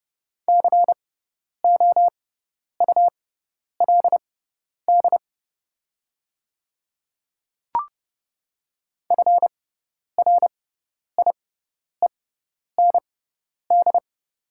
Morse code, heard as a keyed tone, about 30 wpm, spaced as for 10 wpm, 700 Hz. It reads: COULD FRIEND
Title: 6x Character Spacing - Top 300 Words - 30wpm